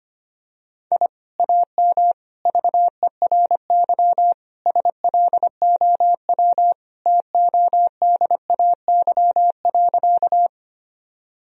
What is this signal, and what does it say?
Morse code, heard as a keyed tone, about 25 wpm, 700 Hz. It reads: I AM VERY SLOW TODAY.
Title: Sentences from Top 1000 Words - Review Part 2 - 25wpm